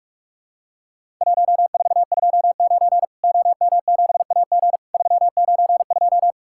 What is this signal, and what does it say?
Morse code, heard as a keyed tone, about 48 wpm, 700 Hz. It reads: 1429 OM7AG 392
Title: Worked All Europe DX Contest - QTC Exchange - 45wpm